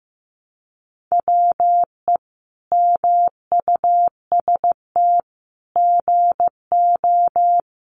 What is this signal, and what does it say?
Morse code, heard as a keyed tone, about 15 wpm, 700 Hz. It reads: WE MUST GO
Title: Sentences from Top 800 Words 15wpm